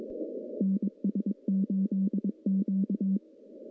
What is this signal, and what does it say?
Morse code, heard as a keyed tone, about 22 wpm, 200 Hz. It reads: NS8Q
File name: contests / CW-SS-k5tr- -0400